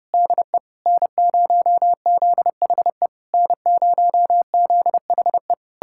Morse code, heard as a keyed tone, about 33 wpm, 700 Hz. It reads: DE N0ZHE N0ZHE